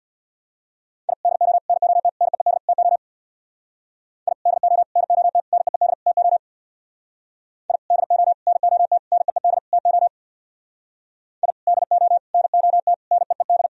Morse code, heard as a keyed tone, about 64 wpm, 700 Hz. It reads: I DO NOT NEED TO I DO NOT NEED TO I DO NOT NEED TO I DO NOT NEED